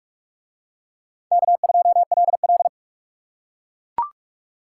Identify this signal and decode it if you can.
Morse code, heard as a keyed tone, about 38 wpm, 700 Hz. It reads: K2LL